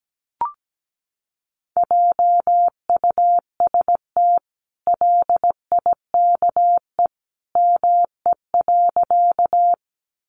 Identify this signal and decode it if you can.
Morse code, heard as a keyed tone, about 17 wpm, 700 Hz. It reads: JUST LIKE ME.
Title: Sentences from Top 200 Words 17wpm